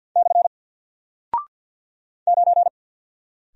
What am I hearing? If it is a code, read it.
Morse code, heard as a keyed tone, about 48 wpm, 700 Hz. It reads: / 9
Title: Single Character (Letter:Number:Punctuation:Prosign) Alt 50wpm